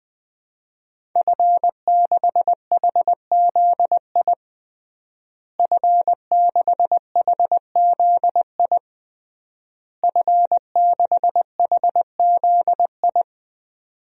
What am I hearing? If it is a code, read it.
Morse code, heard as a keyed tone, about 21 wpm, 700 Hz. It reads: F6HZI F6HZI F6HZI
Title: DXCC-100 Call Signs - 5 Characters - Repeated 3x - 20wpm